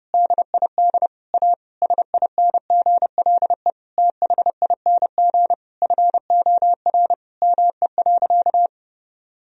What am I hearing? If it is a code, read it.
Morse code, heard as a keyed tone, about 30 wpm, 700 Hz. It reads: DID A SINGLE THING FOR ME.
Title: Sentences from Top 1000 Words - Review Part 4 - 30wpm